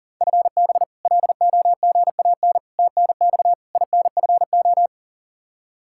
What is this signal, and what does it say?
Morse code, heard as a keyed tone, about 40 wpm, 700 Hz. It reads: FB LOGAN TNX INFO
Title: K1USN SST QSO Practice 40wpm